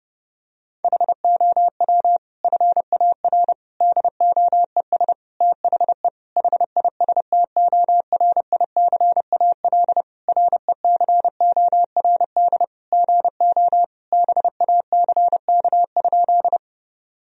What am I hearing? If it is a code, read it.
Morse code, heard as a keyed tone, about 30 wpm, 700 Hz. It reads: HOW FAR DOES THE HISTORICAL RECORD GO BACK?
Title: Sentences from Top 1000 Words - Review Part 4 - 30wpm